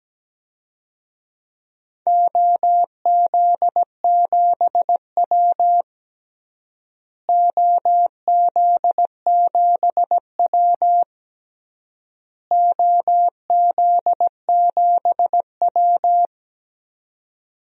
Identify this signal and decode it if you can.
Morse code, heard as a keyed tone, about 17 wpm, 700 Hz. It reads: OZ7W OZ7W OZ7W
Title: DXCC-100 Call Signs - 4 Characters - Repeated 3x - 17wpm